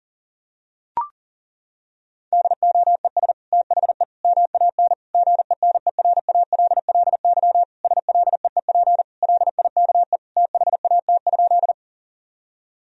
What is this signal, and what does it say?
Morse code, heard as a keyed tone, about 40 wpm, 700 Hz. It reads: DOES THE MAN GENERALLY SLEEP LIKE THAT?
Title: Sentences from Top 1000 Words - Review Part 4 - 40wpm